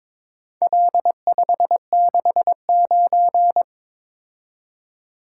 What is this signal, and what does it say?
Morse code, heard as a keyed tone, about 22 wpm, 700 Hz. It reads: L569
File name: practice / 22wpm / Four Letter-Number Combinations - 22wpm